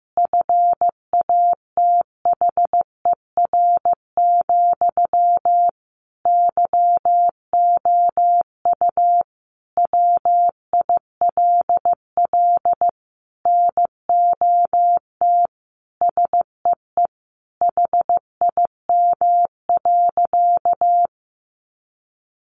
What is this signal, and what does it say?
Morse code, heard as a keyed tone, about 15 wpm, 700 Hz. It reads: FATHER, YOU WILL NOT SEE HIM.
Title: Sentences from Top 600 Words 15wpm